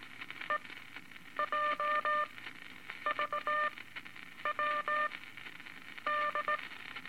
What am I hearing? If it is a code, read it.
Morse code, heard as a keyed tone, about 19 wpm, spaced as for 8 wpm, 1200 Hz, with noise. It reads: EJVW D
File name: Morse Code Records / 17A LCM 162 Eight Words Per Minute